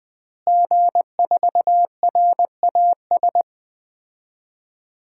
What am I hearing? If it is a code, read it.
Morse code, heard as a keyed tone, about 20 wpm, 700 Hz. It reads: G4RAS
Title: DXCC-100 Call Signs - 5 Characters - Repeated 2x - 20wpm